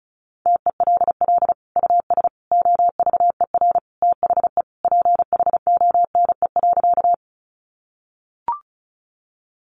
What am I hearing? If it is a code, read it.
Morse code, heard as a keyed tone, about 34 wpm, 700 Hz. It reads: TELL US OVER THE PHONE.
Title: Sentences from Top 1100 Words 35wpm